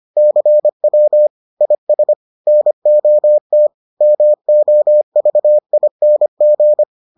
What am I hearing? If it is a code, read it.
Morse code, heard as a keyed tone, about 25 wpm, 600 Hz. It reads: CW IS NOT MOVING